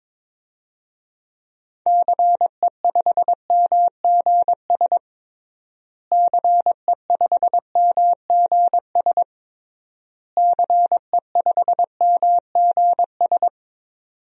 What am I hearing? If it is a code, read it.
Morse code, heard as a keyed tone, about 22 wpm, 700 Hz. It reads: CE5MGS CE5MGS CE5MGS
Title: DXCC-100 Call Signs - 6 Characters - Repeated 3x - 22wpm